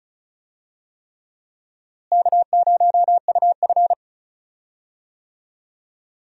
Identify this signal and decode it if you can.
Morse code, heard as a keyed tone, about 36 wpm, 700 Hz. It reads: K0UF
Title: USA Call Signs - 4 Characters - Repeated 3x - 35wpm